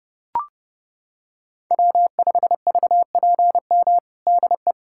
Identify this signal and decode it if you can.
Morse code, heard as a keyed tone, about 30 wpm, 700 Hz. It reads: W5VPM DE
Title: USA Callsign DE Callsign - 5 Characters - 30wpm